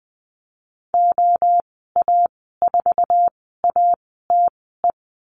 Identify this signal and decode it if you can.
Morse code, heard as a keyed tone, about 19 wpm, spaced as for 14 wpm, 700 Hz. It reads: OA4ATE
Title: 140 Lesson 23g - Letter f - Call Signs 20wpm